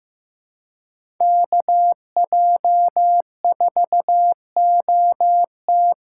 Morse code, heard as a keyed tone, about 15 wpm, 700 Hz. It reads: KJ4OT
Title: USA Call Signs - 6 Characters - 6x Character Spacing - 15wpm